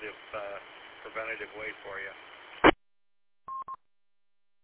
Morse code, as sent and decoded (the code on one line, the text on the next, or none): none